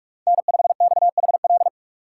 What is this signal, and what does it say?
Morse code, heard as a keyed tone, about 46 wpm, 700 Hz. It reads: N5XHL
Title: USA Call Signs - 4x Character Spacing - 45wpm